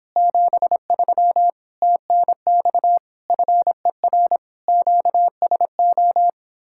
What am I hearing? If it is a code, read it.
Morse code, heard as a keyed tone, about 26 wpm, 700 Hz. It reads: 73 TNX FER QSO